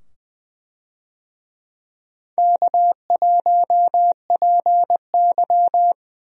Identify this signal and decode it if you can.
Morse code, heard as a keyed tone, about 20 wpm, 700 Hz. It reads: K1PY
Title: USA Call Signs - 4 Characters - 20wpm